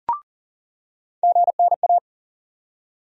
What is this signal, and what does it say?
Morse code, heard as a keyed tone, about 40 wpm, 700 Hz. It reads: GNA
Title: USA Call Sign Postfixes - 3 Letters 40wpm